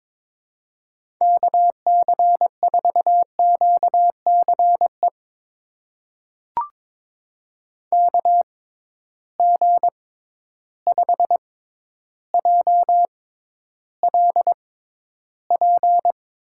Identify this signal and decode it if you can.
Morse code, heard as a keyed tone, about 22 wpm, spaced as for 11 wpm, 700 Hz. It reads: KC4QCE KG5JLP
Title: USA Call Signs - 6 Characters - 6x Character Spacing - 22wpm